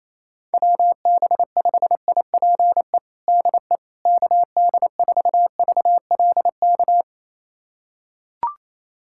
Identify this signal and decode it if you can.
Morse code, heard as a keyed tone, about 28 wpm, 700 Hz. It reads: WB5IPE DE KD4VLK
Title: USA Callsign DE Callsign 28wpm